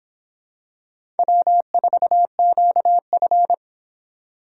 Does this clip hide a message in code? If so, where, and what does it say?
Yes: Morse code from 1.2 to 3.5 s, heard as a keyed tone, about 24 wpm, 700 Hz. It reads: W4QF